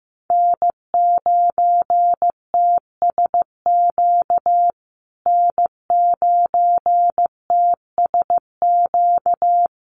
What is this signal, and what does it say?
Morse code, heard as a keyed tone, about 15 wpm, 700 Hz. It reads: N9TSQ N9TSQ
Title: USA Call Signs Repeated 2x 15wpm